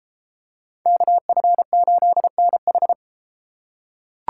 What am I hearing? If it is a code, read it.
Morse code, heard as a keyed tone, about 31 wpm, 700 Hz. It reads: KF8NH